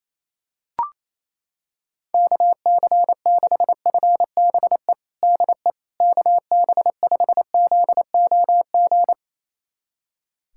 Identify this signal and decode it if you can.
Morse code, heard as a keyed tone, about 28 wpm, 700 Hz. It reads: KC6FBE DE KB5ZOG
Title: USA Callsign DE Callsign - 6 Characters - 28wpm